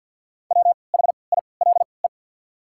Morse code, heard as a keyed tone, about 54 wpm, spaced as for 29 wpm, 700 Hz. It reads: WHILE
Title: 3x Character Spacing - Top 500 Words - 50wpm